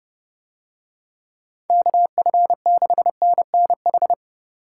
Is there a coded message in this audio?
Yes: Morse code from 1.7 to 4.1 s, heard as a keyed tone, about 30 wpm, 700 Hz. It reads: KF6NNH